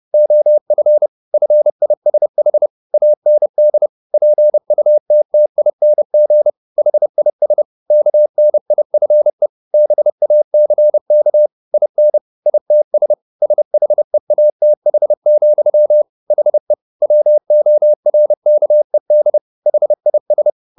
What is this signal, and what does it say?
Morse code, heard as a keyed tone, about 30 wpm, 600 Hz. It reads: OF FISH AND PUTTING HIS KNIFE BACK IN ITS SHEATH, HE WORKED HIS